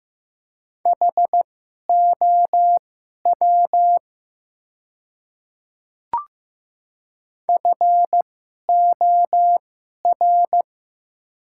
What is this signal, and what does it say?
Morse code, heard as a keyed tone, about 15 wpm, spaced as for 11 wpm, 700 Hz. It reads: HOW FOR
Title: 2x Character Spacing - Top 100 Words - 15wpm